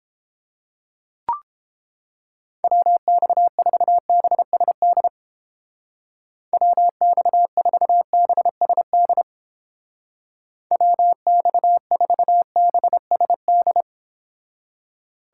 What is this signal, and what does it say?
Morse code, heard as a keyed tone, about 29 wpm, 700 Hz. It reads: WX4BSD WX4BSD WX4BSD